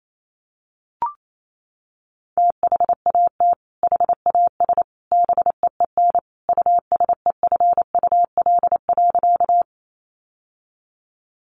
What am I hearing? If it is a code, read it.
Morse code, heard as a keyed tone, about 28 wpm, 700 Hz. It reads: THAT HAS BEEN USEFUL.
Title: Sentences from Top 700 Words 28wpm